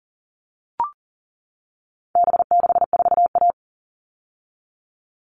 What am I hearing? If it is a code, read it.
Morse code, heard as a keyed tone, about 40 wpm, 700 Hz. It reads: B64A
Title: All Cut Numbers - 4-Digit 40wpm